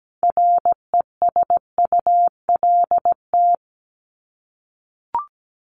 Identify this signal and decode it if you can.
Morse code, heard as a keyed tone, about 17 wpm, 700 Hz. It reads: RESULT
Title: ES:S Suffix - Top 100 Words with Base Words 17wpm